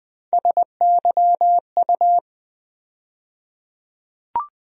Morse code, heard as a keyed tone, about 21 wpm, 700 Hz. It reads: SYU
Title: USA Call Sign Postfixes - 3 Letters 20wpm